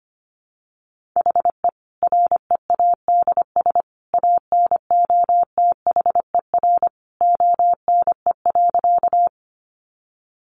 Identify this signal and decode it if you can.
Morse code, heard as a keyed tone, about 25 wpm, 700 Hz. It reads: HE READS ANOTHER ONE.
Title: Additional Easy Phrases 25wpm